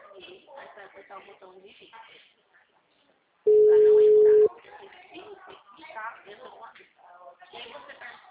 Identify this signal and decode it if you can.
Ringback tone (Europe, 425 Hz), with speech.